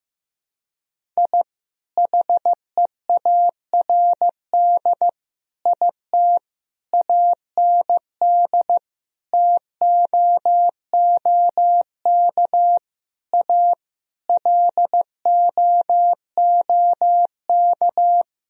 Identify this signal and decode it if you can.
Morse code, heard as a keyed tone, about 16 wpm, 700 Hz. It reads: I HEARD IT AND TOOK A LOOK
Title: Sentences from Top 1000 Words - Review Part 1 - 15wpm